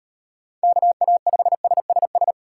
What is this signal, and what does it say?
Morse code, heard as a keyed tone, about 38 wpm, 700 Hz. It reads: KA5SSS